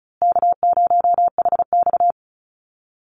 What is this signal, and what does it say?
Morse code, heard as a keyed tone, about 34 wpm, 700 Hz. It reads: K0HX
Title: Easy Call Signs Repeated 3x - 35wpm